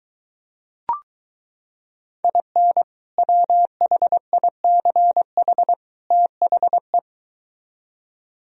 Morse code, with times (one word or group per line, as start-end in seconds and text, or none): 2.2-2.8 IN
3.2-5.7 WHICH
6.1-7.0 THE